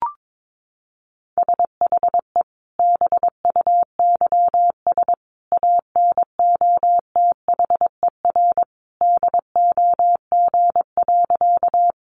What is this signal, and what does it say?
Morse code, heard as a keyed tone, about 22 wpm, 700 Hz. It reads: SHE BUYS ANOTHER DOG.
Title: Easy Phrases 22wpm